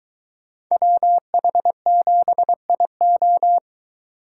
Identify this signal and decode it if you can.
Morse code, heard as a keyed tone, about 22 wpm, 700 Hz. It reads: WH7IO